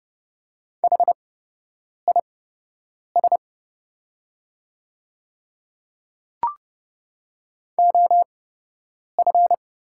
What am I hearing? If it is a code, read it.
Morse code, heard as a keyed tone, about 30 wpm, spaced as for 8 wpm, 690 Hz. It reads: HIS OF
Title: 8x Character Spacing - Top 200 Words - 30wpm